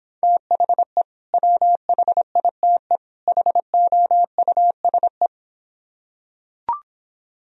Morse code, heard as a keyed tone, about 26 wpm, 700 Hz. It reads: THE WHITE HOUSE